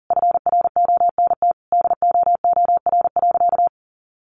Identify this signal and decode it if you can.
Morse code, heard as a keyed tone, about 40 wpm, 700 Hz. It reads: FRONT DOOR.
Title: Sentences from Top 1300 Words 40wpm